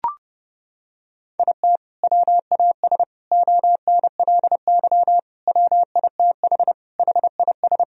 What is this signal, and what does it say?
Morse code, heard as a keyed tone, about 30 wpm, 700 Hz. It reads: IT WAS ONLY WITH HIS
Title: Sentences from Top 1000 Words - Review Part 1 - 30wpm